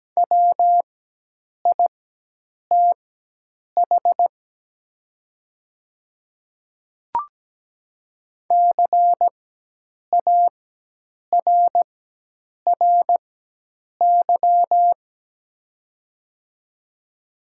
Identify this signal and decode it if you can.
Morse code, heard as a keyed tone, about 17 wpm, spaced as for 8 wpm, 700 Hz. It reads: WITH CARRY
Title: 4x Character Spacing - Top 300 Words - 17wpm